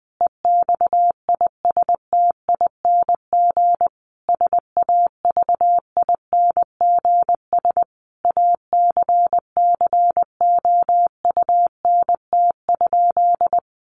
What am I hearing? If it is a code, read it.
Morse code, heard as a keyed tone, about 19 wpm, 700 Hz. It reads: EXISTING SAVINGS ACCOUNT?
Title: Sentences from Top 1500 Words 20wpm